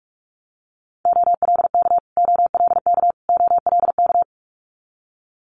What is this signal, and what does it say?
Morse code, heard as a keyed tone, about 46 wpm, 700 Hz. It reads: OLK OLK OLK
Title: Multi-letter Phonemes Repeated 3x 45wpm